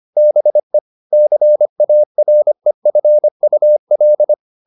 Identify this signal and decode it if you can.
Morse code, heard as a keyed tone, about 25 wpm, 600 Hz. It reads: BE CAREFUL
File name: the old man and the sea part 1 25wpm